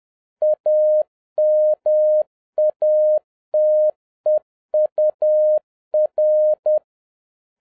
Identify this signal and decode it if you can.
Morse code, heard as a keyed tone, about 10 wpm, 610 Hz. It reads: AMATEUR